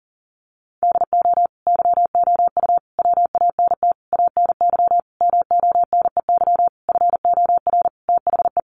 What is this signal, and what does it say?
Morse code, heard as a keyed tone, about 40 wpm, 700 Hz. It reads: DO YOU WANT ANY MONEY FOR THE
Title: Sentences from Top 1400 Words 40wpm